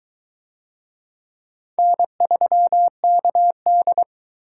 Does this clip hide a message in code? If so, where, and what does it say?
Yes: Morse code from 1.8 to 4.0 s, heard as a keyed tone, about 21 wpm, 700 Hz. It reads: N3KD